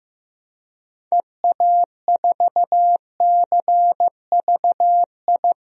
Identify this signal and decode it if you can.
Morse code, heard as a keyed tone, about 15 wpm, 700 Hz. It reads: EA4CVI
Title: DXCC-100 Call Signs - 6 Characters - Repeated 2x - 15wpm